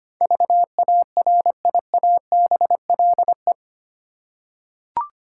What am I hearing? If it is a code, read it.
Morse code, heard as a keyed tone, about 25 wpm, 700 Hz. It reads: VARIABLE
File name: Eight Letter Words - 25wpm